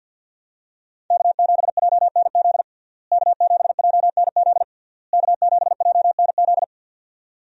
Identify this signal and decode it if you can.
Morse code, heard as a keyed tone, about 52 wpm, 700 Hz. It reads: K7JNZ K7JNZ K7JNZ
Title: USA Call Signs - 5 Characters - Repeated 3x - 50wpm